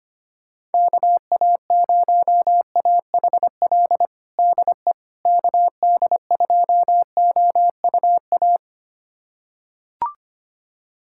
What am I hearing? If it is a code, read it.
Morse code, heard as a keyed tone, about 25 wpm, 700 Hz. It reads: KA0AHL DE KD2OUA
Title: USA Callsign DE Callsign 25wpm